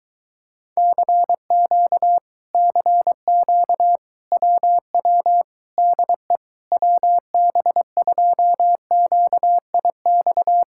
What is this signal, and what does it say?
Morse code, heard as a keyed tone, about 24 wpm, 700 Hz. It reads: CQ CQ WW DE WB2QIX